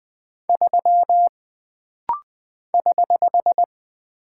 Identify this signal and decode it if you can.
Morse code, heard as a keyed tone, about 20 wpm, 700 Hz. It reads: 3 <HH>
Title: Single Character (Letter:Number:Punctuation:Prosign) - Rapid-Fire - 20wpm